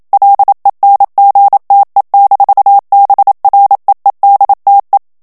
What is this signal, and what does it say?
Morse code, heard as a keyed tone, about 27 wpm, 800 Hz. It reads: LENGTE-BREEDTE
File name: HOLLANDAIS texte clair 1 vitesse 26 mots minute